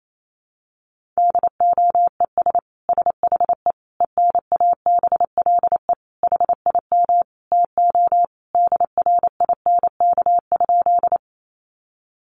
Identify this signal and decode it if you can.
Morse code, heard as a keyed tone, about 28 wpm, 700 Hz. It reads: DOES SHE ENABLE HIM TO DRINK?
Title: Sentences from Top 1500 Words 28wpm